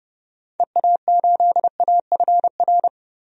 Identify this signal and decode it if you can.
Morse code, heard as a keyed tone, about 31 wpm, 700 Hz. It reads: EA8AFR
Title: DXCC-100 Call Signs - 6 Characters - 30wpm